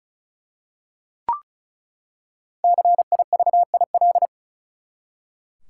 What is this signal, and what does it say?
Morse code, heard as a keyed tone, about 36 wpm, 700 Hz. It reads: CIVIL
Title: Five Letter Words - 35wpm